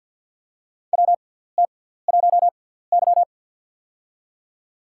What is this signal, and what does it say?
Morse code, heard as a keyed tone, about 47 wpm, spaced as for 17 wpm, 700 Hz. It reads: WT1Y
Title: USA Call Signs - 6x Character Spacing - 50wpm